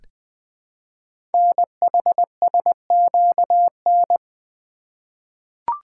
Morse code, heard as a keyed tone, about 20 wpm, 700 Hz. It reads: NHSQN
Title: Code Groups - 5 Letter:Numbers - 20wpm